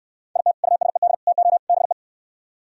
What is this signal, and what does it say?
Morse code, heard as a keyed tone, about 68 wpm, 700 Hz. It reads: IT USED TO BE